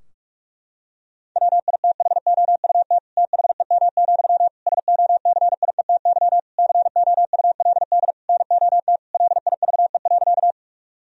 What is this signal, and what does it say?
Morse code, heard as a keyed tone, about 46 wpm, 700 Hz. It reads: WITHOUT THEM, SOCIETY COULD NOT LIVE.
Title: Sentences from Top 1000 Words - Review Part 4 - 45wpm